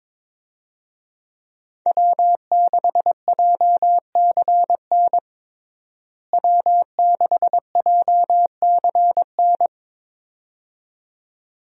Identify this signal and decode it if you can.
Morse code, heard as a keyed tone, about 22 wpm, 700 Hz. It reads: W6JCN W6JCN
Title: USA Call Signs - 5 Characters - Repeated 2x - 22wpm